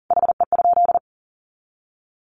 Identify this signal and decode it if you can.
Morse code, heard as a keyed tone, about 40 wpm, 700 Hz. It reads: HE?
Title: Sentences from Top 1300 Words 40wpm